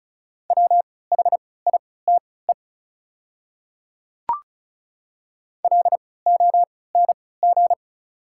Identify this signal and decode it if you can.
Morse code, heard as a keyed tone, about 36 wpm, spaced as for 20 wpm, 690 Hz. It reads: WHITE LONG
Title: 3x Character Spacing - Top 300 Words - 35wpm